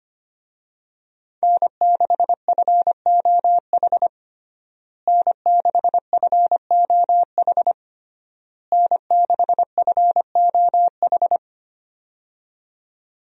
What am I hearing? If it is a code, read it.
Morse code, heard as a keyed tone, about 25 wpm, 700 Hz. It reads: N6FOH N6FOH N6FOH
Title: USA Call Signs - 5 Characters - Repeated 3x - 25wpm